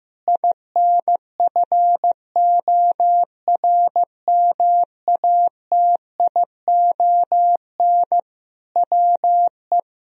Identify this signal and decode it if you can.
Morse code, heard as a keyed tone, about 15 wpm, 700 Hz. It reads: INFORMATION WE
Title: Sentences from Top 1000 Words - Review Part 4 - 15wpm